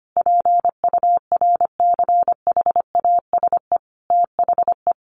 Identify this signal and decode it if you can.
Morse code, heard as a keyed tone, about 25 wpm, 700 Hz. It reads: PURCHASE THE
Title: Sentences from Top 800 Words 25wpm